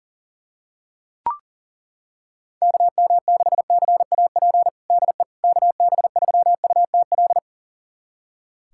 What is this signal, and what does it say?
Morse code, heard as a keyed tone, about 40 wpm, 700 Hz. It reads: KM6CAP DE KB3UTL
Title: USA Callsign DE Callsign 40wpm